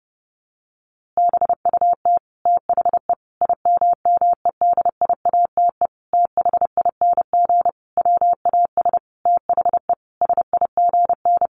Morse code, heard as a keyed tone, about 30 wpm, 700 Hz. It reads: BUT THE IMMEDIATE THING WAS THE SIGN
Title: A Sound Of Thunder 30wpm